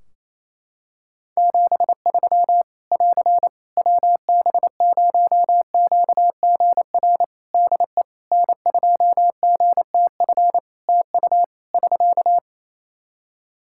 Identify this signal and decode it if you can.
Morse code, heard as a keyed tone, about 27 wpm, 700 Hz. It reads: 73 <AR> WB0QGR DE N2GTF TU <SK>
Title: Practice QSO - Exchange 4 - Ending - 28wpm